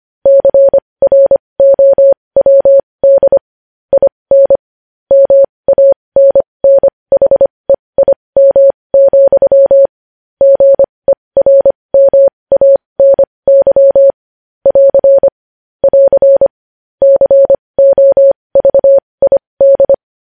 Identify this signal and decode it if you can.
Morse code, heard as a keyed tone, about 25 wpm, spaced as for 20 wpm, 550 Hz. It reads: CROWD IN MANNHEIM, GERMANY <AR> <AR> COVID
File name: podcast / News headlines-1x-25x20